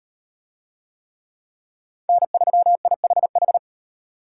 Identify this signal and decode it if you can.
Morse code, heard as a keyed tone, about 35 wpm, 700 Hz. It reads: N3IHH